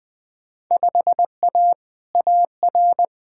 Morse code, heard as a keyed tone, about 20 wpm, 700 Hz. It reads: 5A AR
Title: Field Day Exchanges Encore - 20wpm